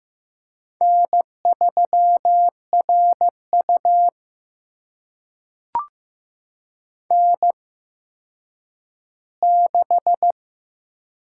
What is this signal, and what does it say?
Morse code, heard as a keyed tone, about 15 wpm, spaced as for 9 wpm, 700 Hz. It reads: N3RU N6
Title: USA Call Signs - 4 Characters - 8x Character Spacing - 15wpm